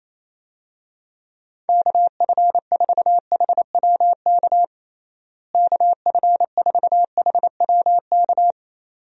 Morse code, heard as a keyed tone, about 28 wpm, 700 Hz. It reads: KF4HWK KF4HWK
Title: USA Call Signs - 6 Characters - Repeated 2x - 28wpm